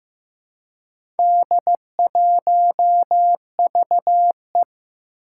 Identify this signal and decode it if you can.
Morse code, heard as a keyed tone, about 15 wpm, 700 Hz. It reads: D1VE
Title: Four Challenging Letter-Number Combinations 15wpm